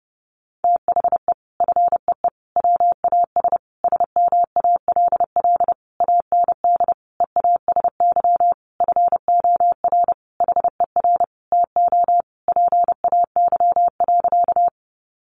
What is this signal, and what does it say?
Morse code, heard as a keyed tone, about 28 wpm, 700 Hz. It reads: THE FEE WAS SMALL AND EASY FOR HER TO PAY.
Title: Sentences from Top 1500 Words 30wpm